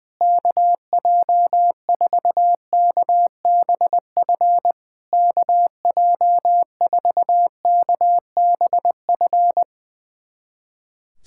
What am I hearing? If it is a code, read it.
Morse code, heard as a keyed tone, about 20 wpm, 700 Hz. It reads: KJ4KBF KJ4KBF
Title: SOTA QSO Practice 20wpm